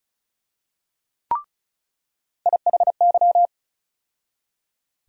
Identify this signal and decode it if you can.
Morse code, heard as a keyed tone, about 34 wpm, 700 Hz. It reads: IHY
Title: USA Call Sign Postfixes - 3 Letters 35wpm